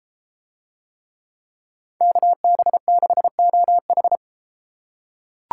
Morse code, heard as a keyed tone, about 31 wpm, 700 Hz. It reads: KB6OH